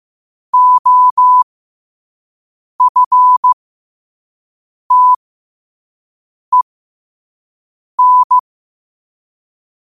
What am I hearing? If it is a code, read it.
Morse code, heard as a keyed tone, about 15 wpm, spaced as for 5 wpm, 1000 Hz. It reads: OFTEN
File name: qotdfiles / 5wpm